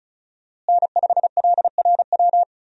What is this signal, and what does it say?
Morse code, heard as a keyed tone, about 34 wpm, 700 Hz. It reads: N5LRW
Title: USA Call Signs - 5 Characters - Repeated 2x - 35wpm